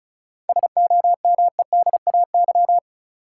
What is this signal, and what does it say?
Morse code, heard as a keyed tone, about 36 wpm, 700 Hz. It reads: SOMEDAY
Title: Sentences from Top 1600 Words - New Word Drill - 35wpm